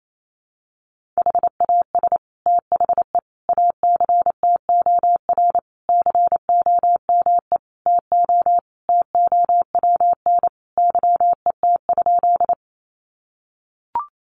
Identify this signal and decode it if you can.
Morse code, heard as a keyed tone, about 28 wpm, 700 Hz. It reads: HAS THE ACTOR COME TO TOWN YET?
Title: Sentences from Top 1500 Words 28wpm